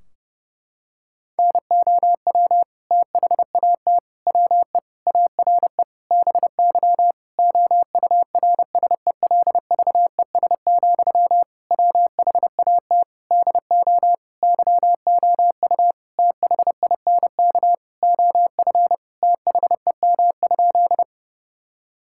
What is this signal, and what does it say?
Morse code, heard as a keyed tone, about 30 wpm, 700 Hz. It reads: NOW THAT WE ARE BY OURSELVES, WHAT DO YOU THINK OF THEM?
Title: Sentences from Top 1600 Words 30wpm